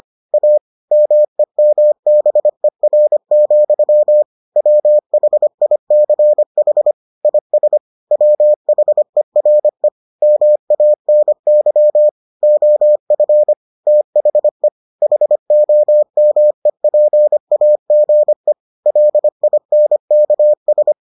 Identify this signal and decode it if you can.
Morse code, heard as a keyed tone, about 25 wpm, 600 Hz. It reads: A MEMBER, WHICH IS WHERE MANY OF THE HOMEPAGE LINKS